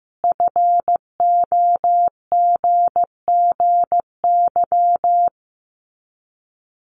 Morse code, heard as a keyed tone, about 15 wpm, 700 Hz. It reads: FOGGY
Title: Common QSO Phrases 15wpm